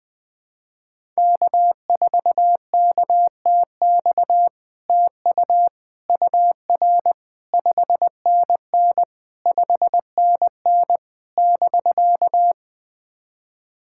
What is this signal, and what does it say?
Morse code, heard as a keyed tone, about 20 wpm, 700 Hz. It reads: K4KTX TU UR 5NN 5NN <BK>
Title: SOTA QSO Practice 20wpm